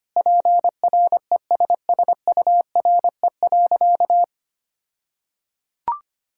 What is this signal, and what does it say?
Morse code, heard as a keyed tone, about 25 wpm, 700 Hz. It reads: PRESSURE.
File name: practice / 25wpm / All-Age Appropriate Jokes - 25wpm